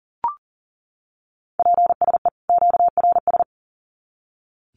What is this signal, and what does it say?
Morse code, heard as a keyed tone, about 40 wpm, 700 Hz. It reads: PSE QRS
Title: Common QSO Phrases 40wpm